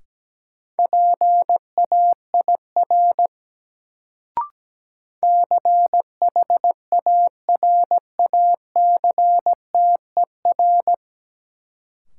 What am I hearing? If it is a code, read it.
Morse code, heard as a keyed tone, about 17 wpm, 700 Hz. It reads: PAIR CHARACTER
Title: Sentences from Top 1200 Words - New Word Drill Alt - 17wpm